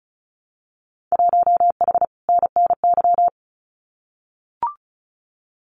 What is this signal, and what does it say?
Morse code, heard as a keyed tone, about 36 wpm, 700 Hz. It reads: 1H NNY
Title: Winter Field Day Exchanges 35wpm